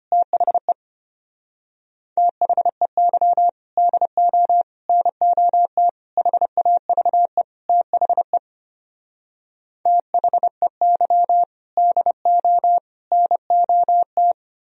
Morse code, heard as a keyed tone, about 33 wpm, 700 Hz. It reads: THE THEY DO NOT HAVE THE THEY DO NOT